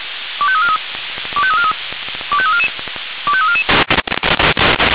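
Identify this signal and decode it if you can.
Five-tone selective call (ZVEI), heard as quick runs of five tones, tones of about 70 ms. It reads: 25343 25343 25344 25344
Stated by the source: ZVEI five-tone call 25212 received on an SDR.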